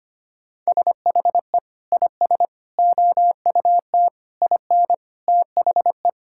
Morse code, heard as a keyed tone, about 25 wpm, 700 Hz. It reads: SHE IS OUT IN THE
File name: practice / 25wpm / Sentences with 3-Word Phrases - 25wpm